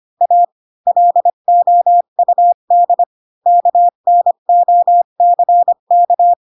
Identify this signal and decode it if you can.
Morse code, heard as a keyed tone, about 25 wpm, 700 Hz. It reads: A LOUD KNOCK
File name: Pinocchio / Pinocchio ch2 20 25 ik1hge